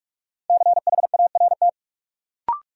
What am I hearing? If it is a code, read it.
Morse code, heard as a keyed tone, about 43 wpm, 700 Hz. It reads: CHART
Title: Sentences from Top 1300 Words - New Word Drill Alt - 45wpm